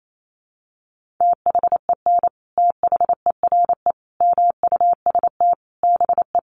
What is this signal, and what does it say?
Morse code, heard as a keyed tone, about 28 wpm, 700 Hz. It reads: THEN THERE MUST BE
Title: Sentences from Top 1400 Words 28wpm